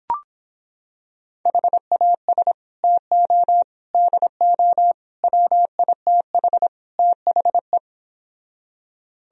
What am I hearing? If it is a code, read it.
Morse code, heard as a keyed tone, about 26 wpm, 700 Hz. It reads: HAS TO DO WITH THE